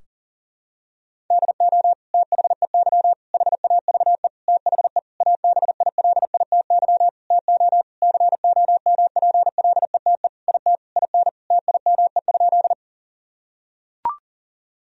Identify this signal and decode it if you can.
Morse code, heard as a keyed tone, about 37 wpm, 700 Hz. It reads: DO THEY HAVE THE ABILITY TO COMPLETE IT IN TIME?
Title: Sentences from Top 1000 Words - Review Part 3 - 40wpm